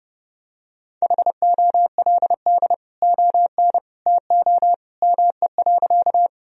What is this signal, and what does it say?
Morse code, heard as a keyed tone, about 30 wpm, 700 Hz. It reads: HOLD ON TO ME.
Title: Sentences from Top 1000 Words - Review Part 2 - 30wpm